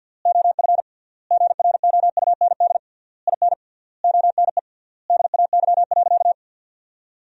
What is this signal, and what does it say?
Morse code, heard as a keyed tone, about 48 wpm, 700 Hz. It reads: OF GROUND IN ONE DAY.
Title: 3x Word-Spacing - Sentences from Top 500 Words - 50wpm